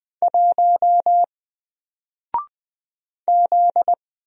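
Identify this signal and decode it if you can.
Morse code, heard as a keyed tone, about 20 wpm, 700 Hz. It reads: 1 Z
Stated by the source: Single letter and number Morse drill at 20 wpm, from N4LCD.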